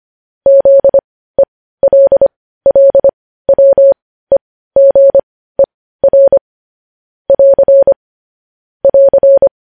Morse code, heard as a keyed tone, about 25 wpm, spaced as for 15 wpm, 550 Hz. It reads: ZELLWEGER <AR> <AR>
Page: News Headlines in Morse Code at 15 WPM